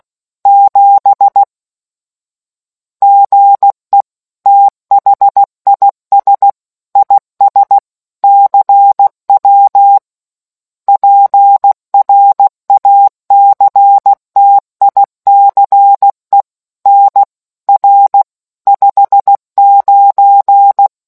Morse code, heard as a keyed tone, about 16 wpm, 800 Hz. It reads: 7 GE THIS IS CW PRACTICE NR 59